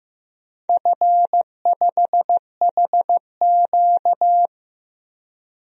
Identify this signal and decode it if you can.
Morse code, heard as a keyed tone, about 15 wpm, 700 Hz. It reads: F5HQ
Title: DXCC-100 Call Signs - 4 Characters - Repeated 2x - 15wpm